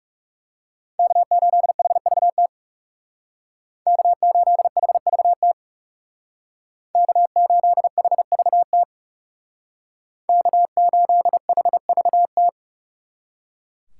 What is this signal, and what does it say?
Morse code, heard as a keyed tone, about 38 wpm, 700 Hz. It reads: K8HVT K8HVT K8HVT K8HVT